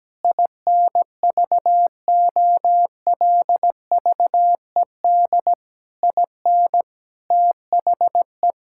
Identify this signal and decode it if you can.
Morse code, heard as a keyed tone, about 17 wpm, 700 Hz. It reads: INVOLVED IN THE
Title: Sentences from Top 1000 Words - Review Part 5 - 17wpm